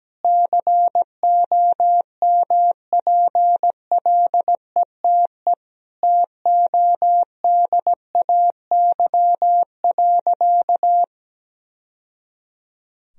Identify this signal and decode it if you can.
Morse code, heard as a keyed tone, about 17 wpm, 700 Hz. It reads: COMPLETE TODAY.
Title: Sentences from Top 1000 Words - Review Part 2 - 17wpm